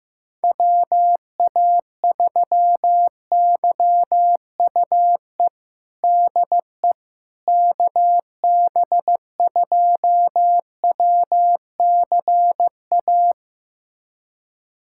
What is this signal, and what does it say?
Morse code, heard as a keyed tone, about 15 wpm, 700 Hz. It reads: WA3YUE DE KB2WCA
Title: USA Callsign DE Callsign 15wpm